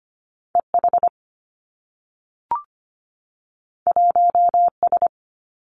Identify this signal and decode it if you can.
Morse code, heard as a keyed tone, about 25 wpm, 700 Hz. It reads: EH 1S
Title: Two Challenging Letter-Number Combinations 25wpm